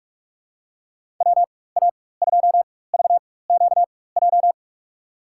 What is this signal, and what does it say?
Morse code, heard as a keyed tone, about 41 wpm, spaced as for 20 wpm, 700 Hz. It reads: WA2VQJ
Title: USA Call Signs - 4x Character Spacing - 45wpm